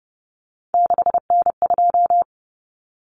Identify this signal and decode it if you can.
Morse code, heard as a keyed tone, about 30 wpm, 700 Hz. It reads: 6N2
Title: Common Cut Numbers - 3-Digit 30wpm